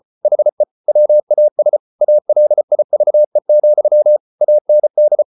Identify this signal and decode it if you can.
Morse code, heard as a keyed tone, about 34 wpm, 600 Hz. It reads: HE WAS ALIVE, AND